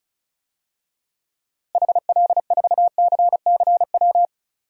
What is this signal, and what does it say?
Morse code, heard as a keyed tone, about 34 wpm, 700 Hz. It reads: HL4CCW
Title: DXCC-100 Call Signs - 6 Characters - Repeated 3x - 35wpm